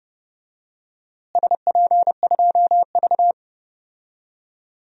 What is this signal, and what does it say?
Morse code, heard as a keyed tone, about 30 wpm, 700 Hz. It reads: SP2V
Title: DXCC-100 Call Signs - 4 Characters - Repeated 3x - 30wpm